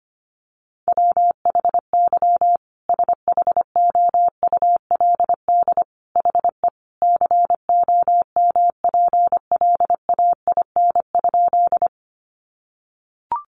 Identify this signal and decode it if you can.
Morse code, heard as a keyed tone, about 25 wpm, 700 Hz. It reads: WHY SHOULD HE COMPLAIN?
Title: Sentences from Top 1300 Words 25wpm